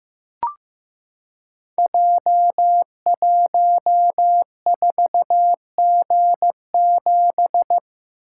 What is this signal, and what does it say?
Morse code, heard as a keyed tone, about 15 wpm, 700 Hz. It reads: J14G7
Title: Code Groups - 5 Letter:Numbers - 15wpm